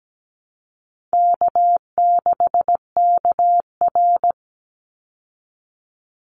Morse code, heard as a keyed tone, about 17 wpm, 700 Hz. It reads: K6KR
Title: Easy Call Signs Repeated 2x - 17wpm